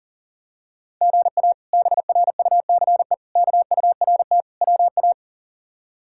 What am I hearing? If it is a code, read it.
Morse code, heard as a keyed tone, about 40 wpm, 700 Hz. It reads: GA BRUCE KURT WA
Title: K1USN SST QSO Practice - 40wpm